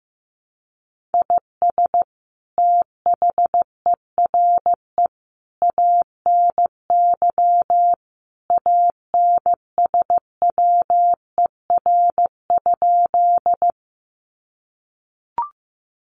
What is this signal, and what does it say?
Morse code, heard as a keyed tone, about 14 wpm, 700 Hz. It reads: IS THERE ANY ANSWER?
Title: Sentences from Top 300 Words 15wpm